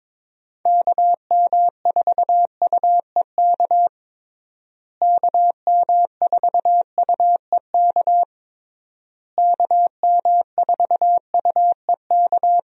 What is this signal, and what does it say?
Morse code, heard as a keyed tone, about 22 wpm, 700 Hz. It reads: KM4UEK KM4UEK KM4UEK
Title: USA Call Signs - 6 Characters - Repeated 3x - 22wpm